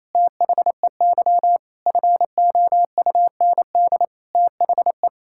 Morse code, heard as a keyed tone, about 28 wpm, 700 Hz. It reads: THEY FOUND THE
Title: Sentences from Top 1000 Words - Review Part 6 - 28wpm